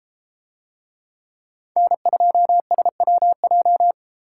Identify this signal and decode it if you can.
Morse code, heard as a keyed tone, about 32 wpm, 700 Hz. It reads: N2SWJ